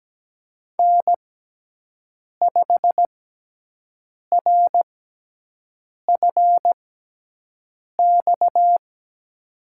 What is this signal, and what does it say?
Morse code, heard as a keyed tone, about 17 wpm, 700 Hz. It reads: N 5 R F X
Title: USA Call Signs - 5 Characters - 6x Character Spacing - 17wpm